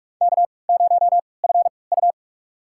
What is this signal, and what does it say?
Morse code, heard as a keyed tone, about 45 wpm, spaced as for 25 wpm, 700 Hz. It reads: K0FU
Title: USA Call Signs - 3x Character Spacing - 45wpm